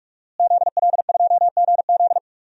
Morse code, heard as a keyed tone, about 44 wpm, 700 Hz. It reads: ZL2GZ
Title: DXCC-100 Call Signs - 5 Characters - Repeated 3x - 45wpm